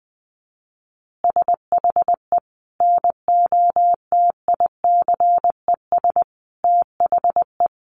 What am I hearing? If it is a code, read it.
Morse code, heard as a keyed tone, about 19 wpm, 700 Hz. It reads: SHE NOTICES THE
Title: Easy Phrases 20wpm